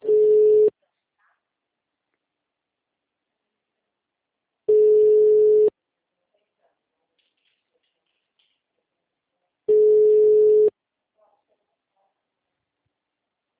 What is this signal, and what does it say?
Ringback tone (Europe, 425 Hz).